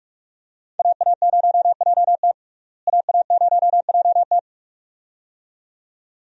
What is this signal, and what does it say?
Morse code, heard as a keyed tone, about 46 wpm, 700 Hz. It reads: AA0JT AA0JT
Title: USA Call Signs - 5 Characters - Repeated 2x - 45wpm